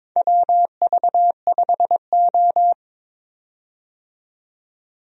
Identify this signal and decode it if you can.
Morse code, heard as a keyed tone, about 22 wpm, 700 Hz. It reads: WV5O
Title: USA Call Signs - 4 Characters - 22wpm